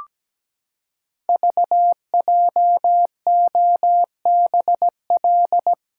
Morse code, heard as a keyed tone, about 17 wpm, 700 Hz. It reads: VJOBL
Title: Code Groups - 5 Letters - 17wpm